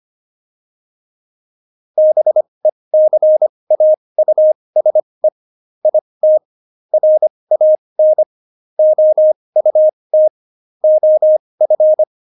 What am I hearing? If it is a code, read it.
Morse code, heard as a keyed tone, about 25 wpm, spaced as for 20 wpm, 620 Hz. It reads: BECAUSE IT RAN OUT OF
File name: Jokes-25-char-20-eff